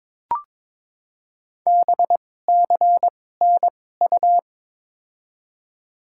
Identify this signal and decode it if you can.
Morse code, heard as a keyed tone, about 22 wpm, spaced as for 16 wpm, 700 Hz. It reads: BCNU
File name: Top 90 QSO Elements - 2x Character Spacing - 22wpm